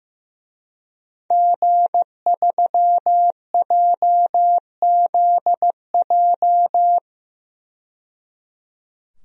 Morse code, heard as a keyed tone, about 15 wpm, 700 Hz. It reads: G3JZJ
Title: DXCC-100 Call Signs - 5 Characters - 15wpm